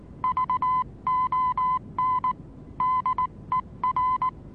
Morse code, heard as a keyed tone, about 19 wpm, 1010 Hz. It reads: VON DER
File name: repos / Morsekurs / DJ4UF Morsen26